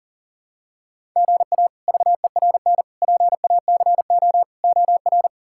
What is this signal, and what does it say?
Morse code, heard as a keyed tone, about 40 wpm, 700 Hz. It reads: GA VERN PACO OR
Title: K1USN SST QSO Practice - 40wpm